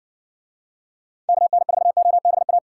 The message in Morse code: KN4OBA

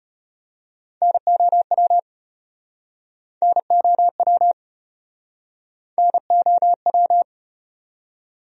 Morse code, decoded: NOW NOW NOW

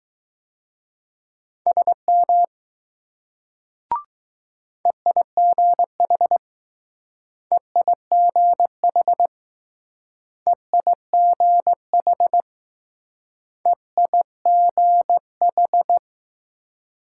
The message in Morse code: SM EIGH EIGH EIGH EIGH

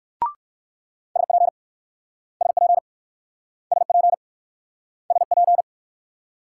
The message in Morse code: SP SP SP SP

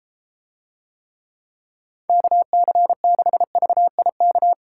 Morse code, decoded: KC6VIK

